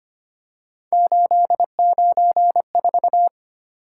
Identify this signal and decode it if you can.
Morse code, heard as a keyed tone, about 25 wpm, 700 Hz. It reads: 894
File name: Three Letter-Number Combinations - 25wpm